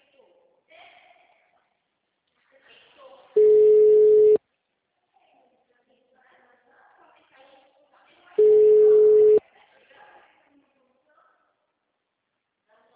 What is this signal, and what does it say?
Ringback tone (Europe, 425 Hz), with speech.
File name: out- -2018- - - .33202